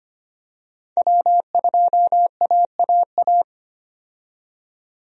Morse code, heard as a keyed tone, about 25 wpm, 700 Hz. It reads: W2AAA
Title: USA Call Signs - 5 Characters - 25wpm